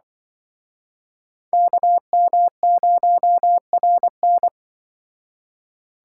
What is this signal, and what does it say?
Morse code, heard as a keyed tone, about 24 wpm, 700 Hz. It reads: KM0RN
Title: Speed Racing - USA 2x2 Call Signs 20wpm